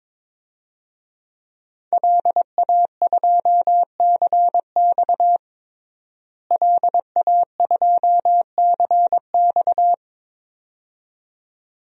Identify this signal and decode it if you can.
Morse code, heard as a keyed tone, about 22 wpm, 700 Hz. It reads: LA2CX LA2CX
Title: DXCC-100 Call Signs - 5 Characters - Repeated 2x - 22wpm